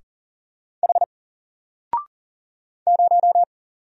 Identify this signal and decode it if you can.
Morse code, heard as a keyed tone, about 40 wpm, 700 Hz. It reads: H 0
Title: Single Letter-Number 40wpm